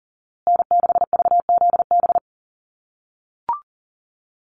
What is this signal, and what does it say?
Morse code, heard as a keyed tone, about 40 wpm, 700 Hz. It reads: N6VZB
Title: USA Call Signs Repeated 3x 40wpm